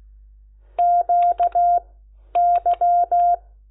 Morse code, heard as a keyed tone, about 16 wpm, spaced as for 10 wpm, 700 Hz. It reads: QY
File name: Lecciones / CLASE Nº10